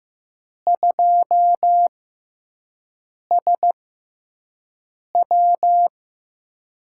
Morse code, heard as a keyed tone, about 15 wpm, spaced as for 5 wpm, 700 Hz. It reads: 2SW
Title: USA Call Signs - 6x Character Spacing - 15wpm